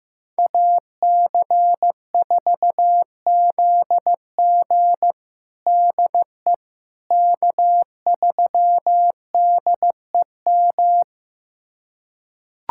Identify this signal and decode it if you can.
Morse code, heard as a keyed tone, about 16 wpm, 700 Hz. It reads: AC4ZG DE K3DEM
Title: USA Callsign DE Callsign - 5 Characters - 15wpm